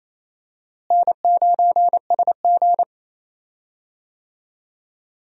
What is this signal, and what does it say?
Morse code, heard as a keyed tone, about 28 wpm, 700 Hz. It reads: N9SG
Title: USA Call Signs - 4 Characters - Repeated 3x - 28wpm